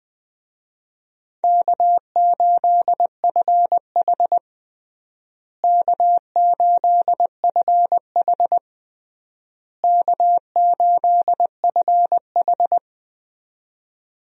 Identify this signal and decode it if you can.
Morse code, heard as a keyed tone, about 21 wpm, 690 Hz. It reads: K8FH K8FH K8FH
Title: USA Call Signs - 4 Characters - Repeated 3x - 20wpm